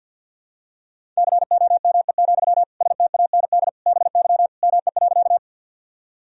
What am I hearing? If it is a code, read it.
Morse code, heard as a keyed tone, about 47 wpm, 700 Hz. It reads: COME, STAND BY ME.
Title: Sentences from Top 1000 Words - Review Part 1 - 50wpm